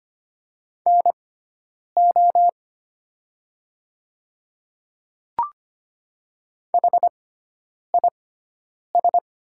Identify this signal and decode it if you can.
Morse code, heard as a keyed tone, about 25 wpm, spaced as for 9 wpm, 700 Hz. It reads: NO HIS
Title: 6x Character Spacing - Top 100 Words - 25wpm